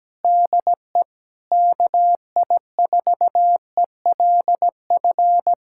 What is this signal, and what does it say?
Morse code, heard as a keyed tone, about 17 wpm, 700 Hz. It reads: DE KI4ELF
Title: Speed-Racing Common QSO Phrases Encore - 17wpm